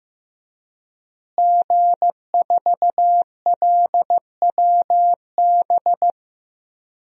Morse code, heard as a keyed tone, about 15 wpm, 700 Hz. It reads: G4LWB
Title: DXCC-100 Call Signs - 5 Characters - Repeated 3x - 15wpm